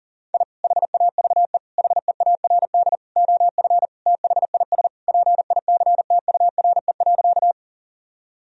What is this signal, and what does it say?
Morse code, heard as a keyed tone, about 40 wpm, 700 Hz. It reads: I HAVE HEARD OF THIS PICTURE.
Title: Sentences from Top 1000 Words - Review Part 1 - 40wpm